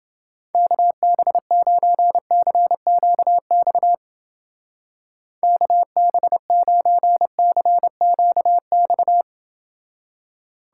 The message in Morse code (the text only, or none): KB9CQX KB9CQX